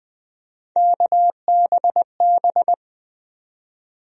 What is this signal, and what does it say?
Morse code, heard as a keyed tone, about 20 wpm, 700 Hz. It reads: KBB
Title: USA Call Sign Postfixes - 3 Letters 20wpm